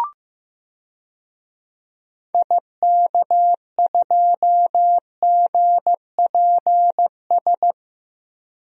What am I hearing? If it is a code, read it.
Morse code, heard as a keyed tone, about 15 wpm, 700 Hz. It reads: IK2GPS
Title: DXCC-100 Call Signs - 6 Characters - Repeated 2x - 15wpm